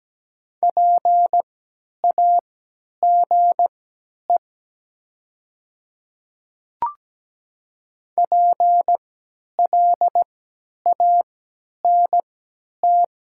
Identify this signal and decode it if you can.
Morse code, heard as a keyed tone, about 17 wpm, spaced as for 10 wpm, 700 Hz. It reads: PAGE PLANT
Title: 3x Character Spacing - Top 500 Words - 17wpm